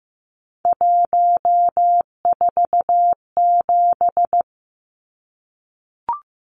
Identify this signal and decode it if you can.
Morse code, heard as a keyed tone, about 15 wpm, 700 Hz. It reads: 147
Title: Numbers - 3-Digit 15wpm